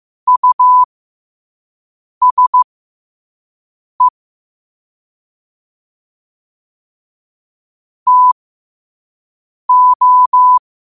Morse code, heard as a keyed tone, about 15 wpm, spaced as for 5 wpm, 1000 Hz. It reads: USE TO
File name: qotdfiles / 5wpm